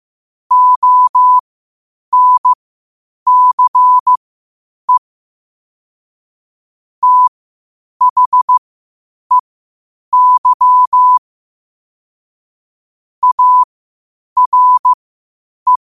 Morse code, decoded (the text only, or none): ONCE THEY ARE